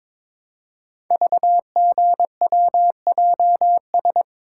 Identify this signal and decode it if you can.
Morse code, heard as a keyed tone, about 22 wpm, 700 Hz. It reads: VGWJS
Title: Code Groups - 5 Letters - 22wpm